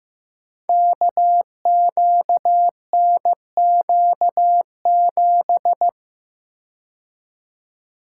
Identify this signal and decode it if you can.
Morse code, heard as a keyed tone, about 16 wpm, 700 Hz. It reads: KQNQ7
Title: Code Groups - 5 Letter:Numbers - 15wpm